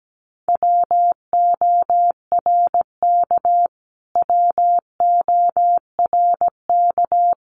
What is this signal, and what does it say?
Morse code, heard as a keyed tone, about 17 wpm, 700 Hz. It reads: WORK WORK
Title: Top 500 Words Repeated 17wpm